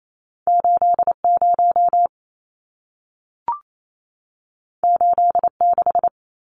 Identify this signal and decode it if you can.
Morse code, heard as a keyed tone, about 28 wpm, 700 Hz. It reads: 80 86
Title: Numbers - 2-Digit 28wpm